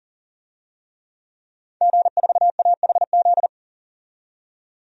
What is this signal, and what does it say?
Morse code, heard as a keyed tone, about 41 wpm, 700 Hz. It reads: G4AHZ
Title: DXCC-100 Call Signs - 5 Characters - 40wpm